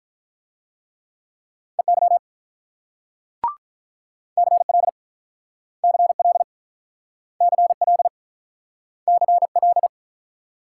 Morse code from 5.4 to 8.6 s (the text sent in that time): CL CL